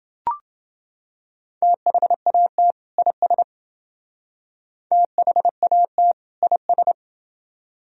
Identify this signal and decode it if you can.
Morse code, heard as a keyed tone, about 28 wpm, 700 Hz. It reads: THAT IS THAT IS